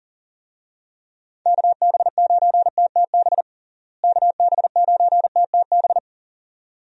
Morse code, heard as a keyed tone, about 37 wpm, 700 Hz. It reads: KB9TTB KB9TTB
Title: USA Call Signs - 6 Characters - Repeated 2x - 40wpm